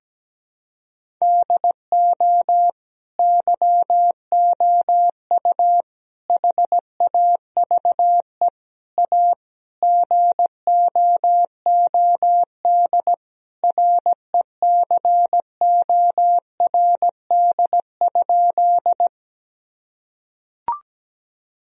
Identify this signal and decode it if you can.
Morse code, heard as a keyed tone, about 17 wpm, 700 Hz. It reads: DO YOU HAVE A GOOD RECORD?
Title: Sentences from Top 1000 Words - Review Part 3 - 17wpm